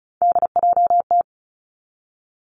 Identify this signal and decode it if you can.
Morse code, heard as a keyed tone, about 36 wpm, 700 Hz. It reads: DJT
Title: Three Challenging Letter-Number Combinations 35wpm